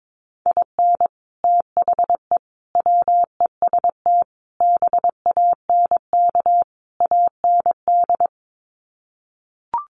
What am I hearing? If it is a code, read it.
Morse code, heard as a keyed tone, about 22 wpm, 700 Hz. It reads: IN THE WEST BANK AND
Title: Sets of 5 Words Encore2 22wpm